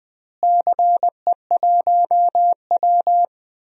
Morse code, heard as a keyed tone, about 20 wpm, 700 Hz. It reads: CE1W
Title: DXCC-100 Call Signs - 4 Characters - Repeated 2x - 20wpm